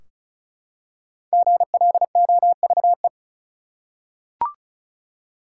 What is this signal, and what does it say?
Morse code, heard as a keyed tone, about 34 wpm, 700 Hz. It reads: GLOVE
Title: Five Letter Words - 35wpm